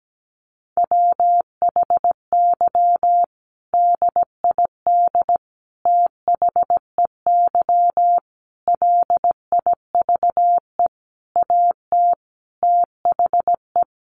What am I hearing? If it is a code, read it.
Morse code, heard as a keyed tone, about 17 wpm, 700 Hz. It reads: WHY DID THEY LIVE AT THE
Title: Sentences from Top 900 Words 17wpm